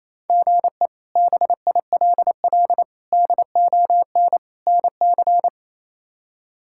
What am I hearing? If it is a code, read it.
Morse code, heard as a keyed tone, about 28 wpm, 700 Hz. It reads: GE BILL DON NC
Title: K1USN SST QSO Practice - 28wpm